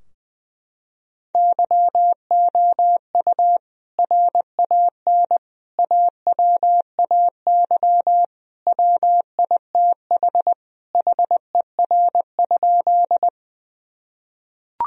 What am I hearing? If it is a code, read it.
Morse code, heard as a keyed tone, about 20 wpm, 700 Hz. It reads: YOU RAN AWAY WITH HER?
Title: Sentences from Top 1000 Words - Review Part 3 - 20wpm